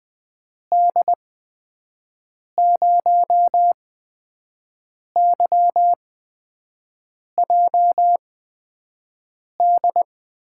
Morse code, heard as a keyed tone, about 21 wpm, spaced as for 5 wpm, 700 Hz. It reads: D0YJD
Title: USA Call Signs - 6 Characters - 8x Character Spacing - 20wpm